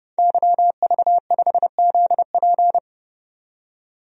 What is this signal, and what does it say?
Morse code, heard as a keyed tone, about 29 wpm, 700 Hz. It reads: YV5ZP
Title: DXCC-100 Call Signs - 5 Characters - 30wpm